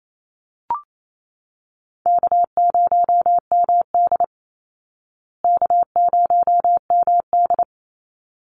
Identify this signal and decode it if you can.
Morse code, heard as a keyed tone, about 28 wpm, 700 Hz. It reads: K0MD K0MD
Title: Easy Call Signs Repeated 2x - 28wpm